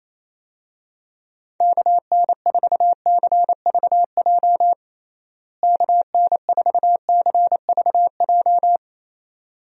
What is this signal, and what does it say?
Morse code, heard as a keyed tone, about 28 wpm, 700 Hz. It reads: KN4CVJ KN4CVJ
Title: USA Call Signs - 6 Characters - Repeated 2x - 28wpm